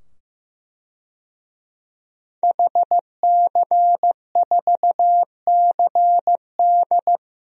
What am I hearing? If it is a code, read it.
Morse code, heard as a keyed tone, about 15 wpm, 700 Hz. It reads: HC4CD
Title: DXCC-100 Call Signs - 5 Characters - 15wpm